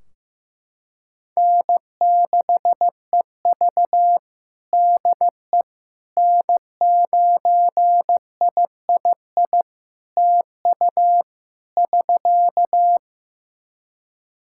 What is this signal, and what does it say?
Morse code, heard as a keyed tone, about 16 wpm, 700 Hz. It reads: N6EV DE N9III TU <SK>
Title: Practice QSO - Exchange 4 - Ending - 15wpm